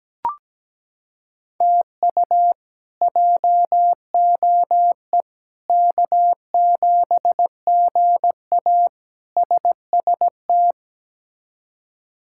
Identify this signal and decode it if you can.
Morse code, heard as a keyed tone, about 17 wpm, 700 Hz. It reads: TU JOE K7GA SST
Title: K1USN SST QSO Practice - 17wpm